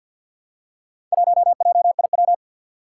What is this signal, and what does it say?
Morse code, heard as a keyed tone, about 46 wpm, 700 Hz. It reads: 1JIW
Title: Four Challenging Letter-Number Combinations 50wpm